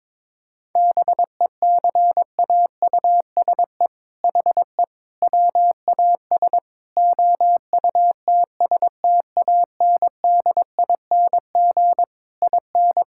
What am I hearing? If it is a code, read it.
Morse code, heard as a keyed tone, about 22 wpm, 700 Hz. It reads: BECAUSE HE WAS OUTSTANDING IN
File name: All-Age Appropriate Jokes - 22wpm